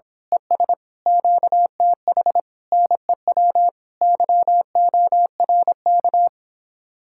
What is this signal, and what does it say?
Morse code, heard as a keyed tone, about 26 wpm, 700 Hz. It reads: ES QTH NEW YORK
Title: Speed-Racing Common QSO Phrases - 22wpm